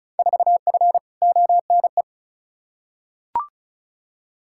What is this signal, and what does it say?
Morse code, heard as a keyed tone, about 36 wpm, 700 Hz. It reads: 4F ONE
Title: Field Day Exchanges Encore - 35wpm